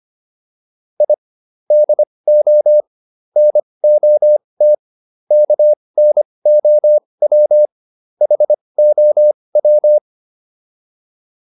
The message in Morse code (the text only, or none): I DO NOT KNOW HOW